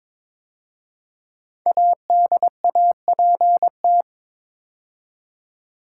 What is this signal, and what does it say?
Morse code, heard as a keyed tone, about 22 wpm, 700 Hz. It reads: ADAPT
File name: 5 Letter Words - First 3 Letters Given - 22wpm